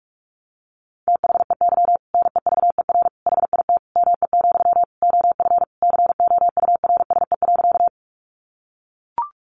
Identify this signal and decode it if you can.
Morse code, heard as a keyed tone, about 44 wpm, 700 Hz. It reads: THEY NEVER HIT ME, OF COURSE.
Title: Sentences from Top 1200 Words 45wpm